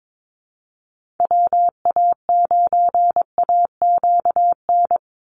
Morse code, heard as a keyed tone, about 22 wpm, 700 Hz. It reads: WA9AQN
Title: Call Signs Repeated 2x - 22wpm